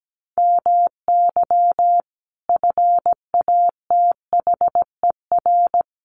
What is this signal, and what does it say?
Morse code, heard as a keyed tone, about 17 wpm, 700 Hz. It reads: MY FATHER
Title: Sentences from Top 1500 Words 17wpm